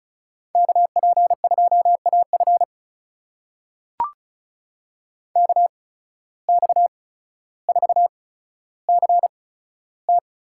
Morse code, heard as a keyed tone, about 34 wpm, spaced as for 15 wpm, 700 Hz. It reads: KP2AF KX4CT
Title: USA Call Signs - 5 Characters - 8x Character Spacing - 35wpm